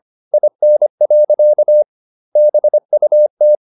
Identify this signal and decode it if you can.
Morse code, heard as a keyed tone, about 25 wpm, 600 Hz. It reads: IN. BUT